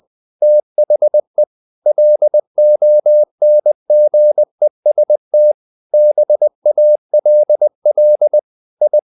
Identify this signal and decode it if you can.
Morse code, heard as a keyed tone, about 20 wpm, 600 Hz. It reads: THE LONGEST BALL I
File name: the old man and the sea / the old man and the sea part 1 20wpm